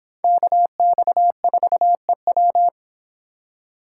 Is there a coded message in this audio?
Yes: Morse code from 0.2 to 2.7 s, heard as a keyed tone, about 26 wpm, 700 Hz. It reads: KX4EW